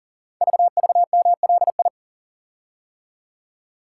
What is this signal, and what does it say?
Morse code, heard as a keyed tone, about 38 wpm, 700 Hz. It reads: VVMLI
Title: Code Groups - 5 Letters - 40wpm